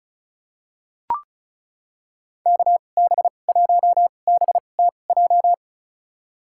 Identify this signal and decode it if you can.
Morse code, heard as a keyed tone, about 36 wpm, spaced as for 25 wpm, 700 Hz. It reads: KB1BTJ
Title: USA Call Signs - 2x Character Spacing - 35wpm